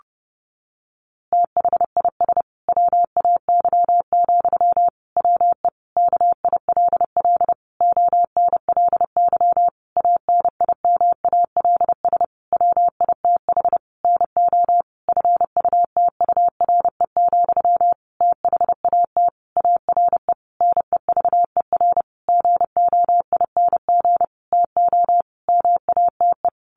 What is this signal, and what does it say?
Morse code, heard as a keyed tone, about 30 wpm, 700 Hz. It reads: THIS WAY, WE KILL ONLY ANIMALS WITH NO FUTURE, THAT ARE NEVER GOING TO MATE
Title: A Sound Of Thunder 30wpm